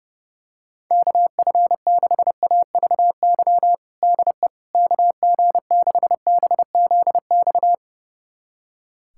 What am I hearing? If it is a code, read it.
Morse code, heard as a keyed tone, about 29 wpm, 700 Hz. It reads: KF6AVY DE KG6BZX
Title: USA Callsign DE Callsign - 6 Characters - 30wpm